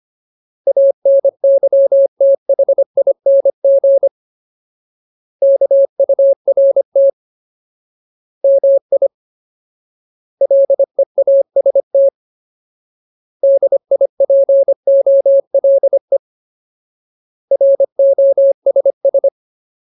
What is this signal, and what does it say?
Morse code, heard as a keyed tone, about 25 wpm, 550 Hz. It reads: ANYTHING KURT MI LEAST DIPOLE ROSS
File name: Ultimate Word List / 25WPM Ultimate